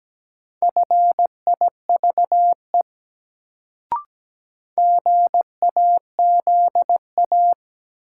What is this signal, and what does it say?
Morse code, heard as a keyed tone, about 17 wpm, 700 Hz. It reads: FIVE GAZA
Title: Sets of 5 Words - New Word Drill - 17wpm